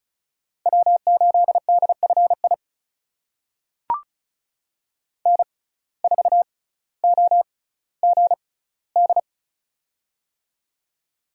Morse code, heard as a keyed tone, about 36 wpm, spaced as for 18 wpm, 700 Hz. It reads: W8DFI N4OGD
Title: USA Call Signs - 5 Characters - 6x Character Spacing - 35wpm